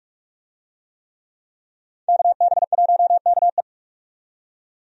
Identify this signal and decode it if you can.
Morse code, heard as a keyed tone, about 43 wpm, 700 Hz. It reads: KB1KE
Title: USA Call Signs - 5 Characters - 45wpm